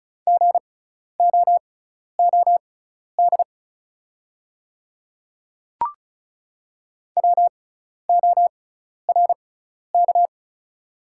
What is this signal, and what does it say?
Morse code, heard as a keyed tone, about 34 wpm, spaced as for 12 wpm, 700 Hz. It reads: GOOD WORK
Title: 6x Character Spacing - Top 200 Words - 35wpm